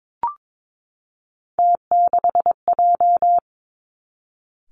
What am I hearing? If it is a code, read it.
Morse code, heard as a keyed tone, about 22 wpm, 700 Hz. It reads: T6J
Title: Three Challenging Letter-Number Combinations 22wpm